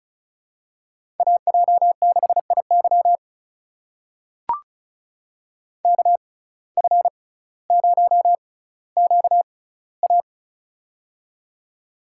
Morse code, heard as a keyed tone, about 36 wpm, spaced as for 18 wpm, 700 Hz. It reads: AJ6IY KF0QA
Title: USA Call Signs - 6x Character Spacing - 35wpm